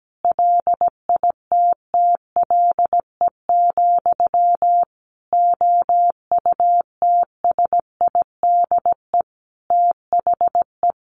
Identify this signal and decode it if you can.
Morse code, heard as a keyed tone, about 17 wpm, 700 Hz. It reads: LITTLE, OUTSIDE THE
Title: Sentences from Top 900 Words 17wpm